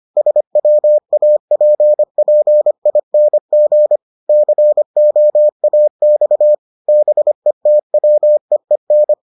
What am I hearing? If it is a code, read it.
Morse code, heard as a keyed tone, about 25 wpm, 600 Hz. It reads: SWAPPING COAX BETWEEN